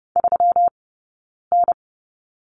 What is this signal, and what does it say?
Morse code, heard as a keyed tone, about 30 wpm, spaced as for 9 wpm, 700 Hz. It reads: 3N
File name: Lesson 20d - Number 3 - Two Characters 30wpm